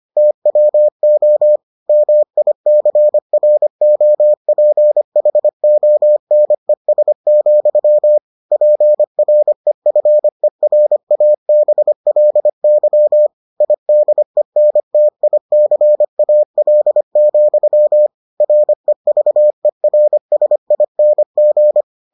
Morse code, heard as a keyed tone, about 25 wpm, 600 Hz. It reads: TWO MICROPHONES, PREFERABLY IDENTICAL, REVERSING